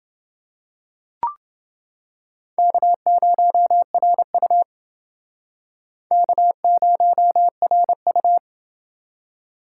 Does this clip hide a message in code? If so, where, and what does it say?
Yes: Morse code from 2.6 to 8.4 s, heard as a keyed tone, about 28 wpm, 700 Hz. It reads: K0RU K0RU